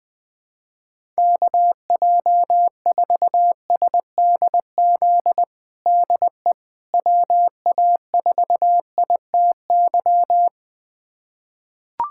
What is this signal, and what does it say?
Morse code, heard as a keyed tone, about 20 wpm, 700 Hz. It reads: KJ4SDZ DE WA4ITY
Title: USA Callsign DE Callsign - 6 Characters - 20wpm